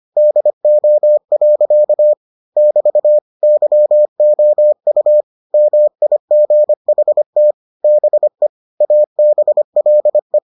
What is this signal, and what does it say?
Morse code, heard as a keyed tone, about 25 wpm, 600 Hz. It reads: DO. <BT> YOU MIGHT BE ABLE